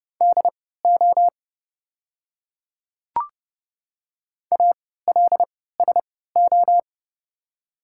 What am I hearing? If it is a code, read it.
Morse code, heard as a keyed tone, about 30 wpm, spaced as for 17 wpm, 700 Hz. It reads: DO ALSO
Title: 3x Character Spacing - Top 200 Words - 30wpm